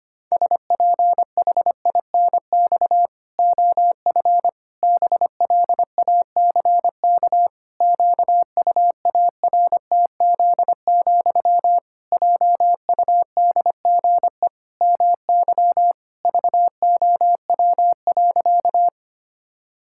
Morse code, heard as a keyed tone, about 25 wpm, 700 Hz. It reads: SPHINX OF BLACK QUARTZ, JUDGE MY VOW.
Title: Pangram Challenge v2 - 25wpm